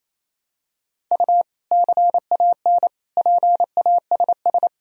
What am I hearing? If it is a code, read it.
Morse code, heard as a keyed tone, about 27 wpm, 700 Hz. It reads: U CAN PASS
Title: Sentences from Top 400 Words and Abbreviations 28wpm